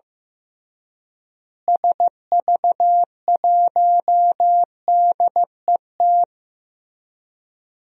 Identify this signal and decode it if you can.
Morse code, heard as a keyed tone, about 15 wpm, 700 Hz. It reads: SV1DET
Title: DXCC-100 Call Signs - 6 Characters - Repeated 3x - 15wpm